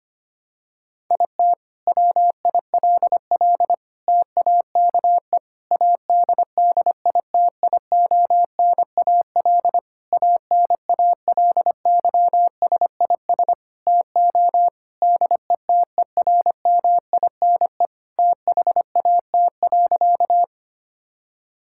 Morse code, heard as a keyed tone, about 25 wpm, 700 Hz. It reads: IT WILL TAKE ADDITIONAL ANALYSIS TO DETERMINE THAT.
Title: Sentences from Top 1000 Words - Review Part 5 - 25wpm